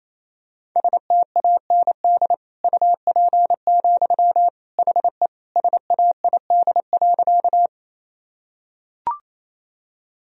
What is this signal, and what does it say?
Morse code, heard as a keyed tone, about 28 wpm, 700 Hz. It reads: STAND UP, HE SAID.
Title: Sentences from Top 1000 Words - Review Part 1 - 28wpm